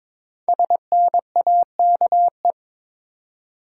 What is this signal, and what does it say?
Morse code, heard as a keyed tone, about 22 wpm, 700 Hz. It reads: SNAKE
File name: 5 Letter Words - First 3 Letters Given - 22wpm